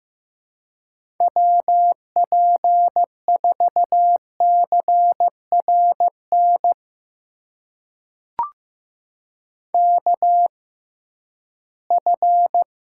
Morse code, heard as a keyed tone, about 15 wpm, spaced as for 11 wpm, 700 Hz. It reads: WP4CRN KF
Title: USA Call Signs - 6 Characters - 6x Character Spacing - 15wpm